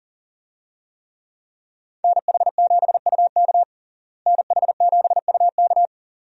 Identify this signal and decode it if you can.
Morse code, heard as a keyed tone, about 40 wpm, 700 Hz. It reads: NH7UK NH7UK
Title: USA Call Signs - 5 Characters - Repeated 2x - 40wpm